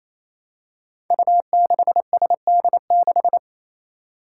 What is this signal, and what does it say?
Morse code, heard as a keyed tone, about 28 wpm, 700 Hz. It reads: U6SD6
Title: Five Challenging Letter-Number Combinations 28wpm